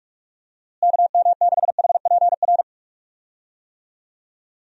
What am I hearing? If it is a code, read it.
Morse code, heard as a keyed tone, about 41 wpm, 700 Hz. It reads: KM6HPR
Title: USA Call Signs - 6 Characters - Repeated 3x - 45wpm